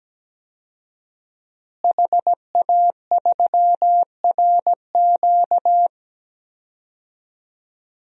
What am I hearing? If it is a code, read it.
Morse code, heard as a keyed tone, about 17 wpm, 700 Hz. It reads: HA3RQ
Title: DXCC-100 Call Signs - 5 Characters - Repeated 2x - 17wpm